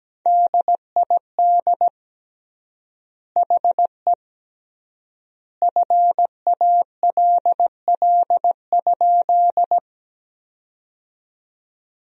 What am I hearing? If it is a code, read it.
Morse code, heard as a keyed tone, about 17 wpm, 700 Hz. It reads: DID HE FALL?
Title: 3x Word-Spacing - Sentences from Top 500 Words - 17wpm